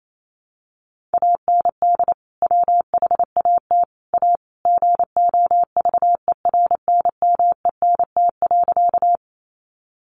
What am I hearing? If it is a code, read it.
Morse code, heard as a keyed tone, about 28 wpm, 700 Hz. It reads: AND WHAT A GOVERNMENT.
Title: Sentences from Top 600 Words 28wpm